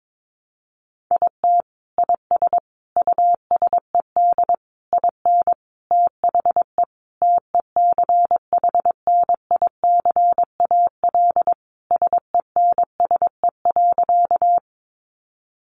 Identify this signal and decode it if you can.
Morse code, heard as a keyed tone, about 22 wpm, 700 Hz. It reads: IT IS USED IN THE TECHNICAL SENSE.
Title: Sentences from Top 1100 Words 22wpm